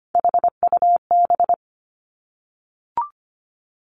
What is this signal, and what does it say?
Morse code, heard as a keyed tone, about 25 wpm, 700 Hz. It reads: HUB
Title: Three Challenging Letter-Number Combinations 25wpm